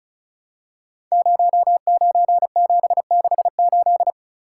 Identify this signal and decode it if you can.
Morse code, heard as a keyed tone, about 36 wpm, 700 Hz. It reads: 09768
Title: Code Groups - 5 Letter:Numbers - 35wpm